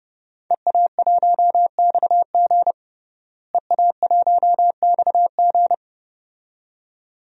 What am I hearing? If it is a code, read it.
Morse code, heard as a keyed tone, about 30 wpm, 700 Hz. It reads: EA1XG EA1XG
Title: DXCC-100 Call Signs - 5 Characters - Repeated 2x - 30wpm